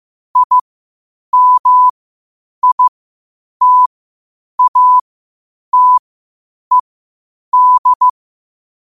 Morse code, heard as a keyed tone, about 15 wpm, spaced as for 8 wpm, 1000 Hz. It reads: IMITATED
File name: qotdfiles / 8wpm